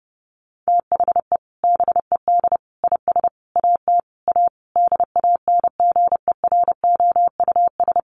Morse code, heard as a keyed tone, about 30 wpm, 700 Hz. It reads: THE BED IS AT A DANGEROUS
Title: Sentences from Top 1500 Words 30wpm